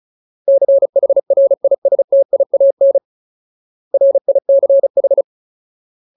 Morse code, heard as a keyed tone, about 34 wpm, 550 Hz. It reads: CHRISTIAN RICH
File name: Ultimate Word List / 35WPM Ultimate